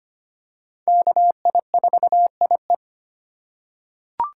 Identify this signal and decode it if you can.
Morse code, heard as a keyed tone, about 25 wpm, 700 Hz. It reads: KI4IE
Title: USA Call Signs - 5 Characters - 4x Character Spacing - 25wpm